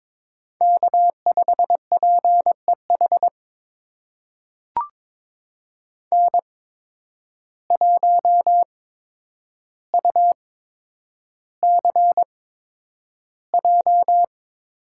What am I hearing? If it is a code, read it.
Morse code, heard as a keyed tone, about 22 wpm, spaced as for 9 wpm, 700 Hz. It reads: K5PEH N1UCJ
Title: USA Call Signs - 5 Characters - 8x Character Spacing - 22wpm